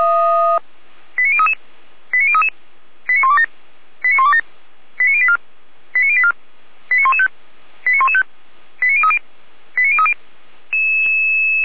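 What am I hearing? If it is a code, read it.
Five-tone selective call (ZVEI), heard as quick runs of five tones, tones of about 70 ms. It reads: 89033 89033 89127 89127 89084 89084 89115 89115 89030 89030
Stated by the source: ZVEI five-tone selective call 89080 received by SDR.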